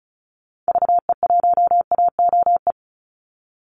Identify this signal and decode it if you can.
Morse code, heard as a keyed tone, about 34 wpm, 700 Hz. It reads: VE1AOE
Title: Call Signs - 35wpm